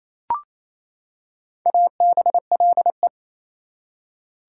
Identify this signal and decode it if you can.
Morse code, heard as a keyed tone, about 28 wpm, 700 Hz. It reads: ABLE
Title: Prefixes and Suffixes 28wpm